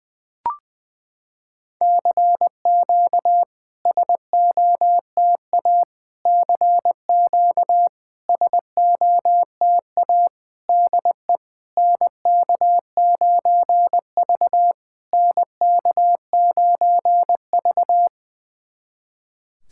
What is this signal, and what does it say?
Morse code, heard as a keyed tone, about 21 wpm, 700 Hz. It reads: CQ SOTA CQ SOTA DE NK9V NK9V
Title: SOTA QSO Practice 20wpm